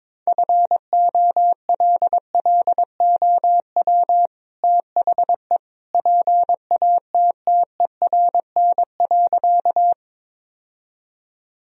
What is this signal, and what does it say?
Morse code, heard as a keyed tone, about 22 wpm, 700 Hz. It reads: FOLLOW THE PATTERN.
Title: Sentences from Top 1000 Words - Review Part 2 - 22wpm